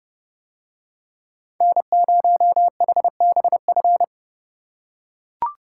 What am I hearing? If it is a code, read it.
Morse code, heard as a keyed tone, about 31 wpm, 700 Hz. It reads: N0HBF